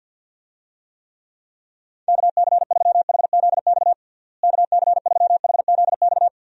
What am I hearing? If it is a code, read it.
Morse code, heard as a keyed tone, about 50 wpm, 700 Hz. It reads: KC3HZX KC3HZX
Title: USA Call Signs - 6 Characters - Repeated 3x - 50wpm